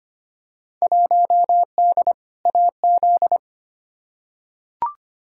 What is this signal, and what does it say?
Morse code, heard as a keyed tone, about 25 wpm, 700 Hz. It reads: 1D AZ
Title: Field Day Exchanges - 25wpm